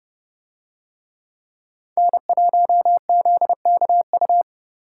Morse code, heard as a keyed tone, about 29 wpm, 700 Hz. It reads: N1ZKU